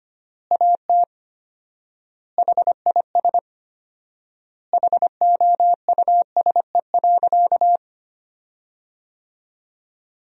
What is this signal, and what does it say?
Morse code, heard as a keyed tone, about 25 wpm, 700 Hz. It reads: AT HIS HOUSE.
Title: 4x Word-Spacing - Sentences from Top 300 Words - 25wpm